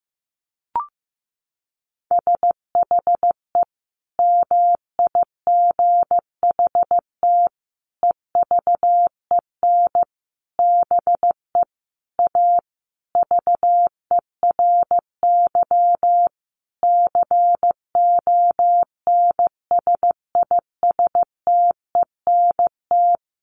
Morse code, heard as a keyed tone, about 14 wpm, 700 Hz. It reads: SHE MIGHT EVEN BE A VERY CONSISTENT
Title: Sentences from Top 1500 Words 15wpm